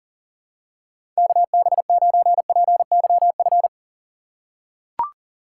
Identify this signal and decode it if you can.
Morse code, heard as a keyed tone, about 40 wpm, 700 Hz. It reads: KB9PYF
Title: USA Call Signs - 6 Characters - 4x Character Spacing - 40wpm